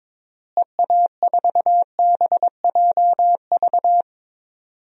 Morse code, heard as a keyed tone, about 21 wpm, 700 Hz. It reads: EA4BJV
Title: DXCC-100 Call Signs - 6 Characters - Repeated 3x - 22wpm